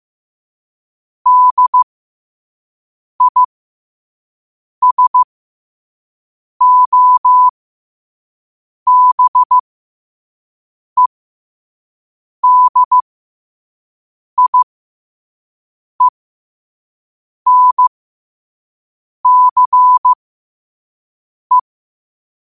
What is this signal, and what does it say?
Morse code, heard as a keyed tone, about 15 wpm, spaced as for 5 wpm, 1000 Hz. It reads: DISOBEDIENCE
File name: qotdfiles / 5wpm